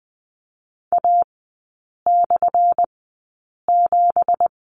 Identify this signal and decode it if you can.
Morse code, heard as a keyed tone, about 20 wpm, spaced as for 8 wpm, 700 Hz. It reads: A/7
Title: 218 Lesson 35e - Number 0 - Three Characters 20wpm